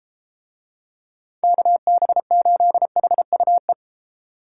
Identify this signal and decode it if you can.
Morse code, heard as a keyed tone, about 31 wpm, 700 Hz. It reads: KB8HUE